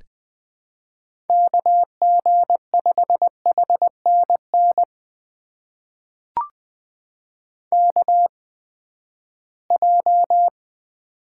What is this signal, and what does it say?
Morse code, heard as a keyed tone, about 20 wpm, spaced as for 14 wpm, 700 Hz. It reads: KG5HNN KJ
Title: USA Call Signs - 6 Characters - 8x Character Spacing - 20wpm